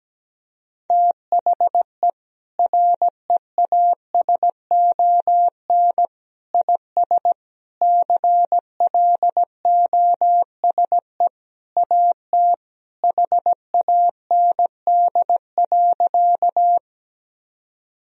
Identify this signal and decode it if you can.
Morse code, heard as a keyed tone, about 17 wpm, 700 Hz. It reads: THE REASON IS CLOSE AT HAND.
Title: Sentences from Top 1000 Words - Review Part 1 - 17wpm